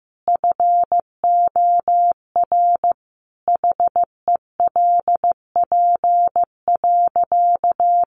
Morse code, heard as a keyed tone, about 15 wpm, 700 Hz. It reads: FOR HELP.
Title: Sentences from Top 1200 Words 15wpm